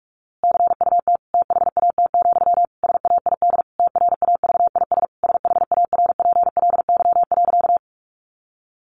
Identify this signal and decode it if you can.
Morse code, heard as a keyed tone, about 44 wpm, 700 Hz. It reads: CUT THAT, SAID TRAVIS SHARPLY.
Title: A Sound Of Thunder 45wpm